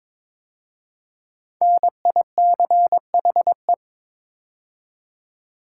Morse code, heard as a keyed tone, about 22 wpm, 700 Hz. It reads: NICHE
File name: 5 Letter Words - First 3 Letters Given - 22wpm